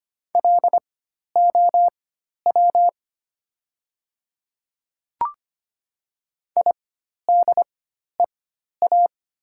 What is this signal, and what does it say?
Morse code, heard as a keyed tone, about 25 wpm, spaced as for 12 wpm, 700 Hz. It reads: LOW IDEA
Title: 4x Character Spacing - Top 300 Words - 25wpm